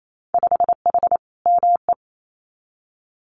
Morse code, heard as a keyed tone, about 28 wpm, 700 Hz. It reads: 5H ME
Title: Winter Field Day Exchanges Encore 28wpm